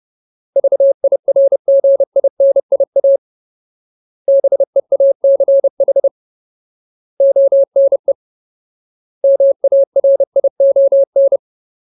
Morse code, heard as a keyed tone, about 30 wpm, 550 Hz. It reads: VIRGINIA BEACH ONE MARION
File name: Ultimate Word List / 30WPM Ultimate